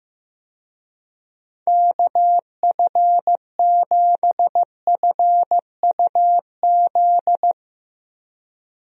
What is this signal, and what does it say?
Morse code, heard as a keyed tone, about 15 wpm, 700 Hz. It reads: KF7FUZ
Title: USA Call Signs - 6 Characters - 15wpm